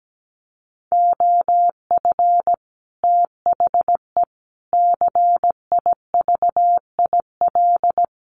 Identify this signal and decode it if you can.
Morse code, heard as a keyed tone, about 17 wpm, 700 Hz. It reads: OF THE CIVIL
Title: Sets of 5 Words Encore2 17wpm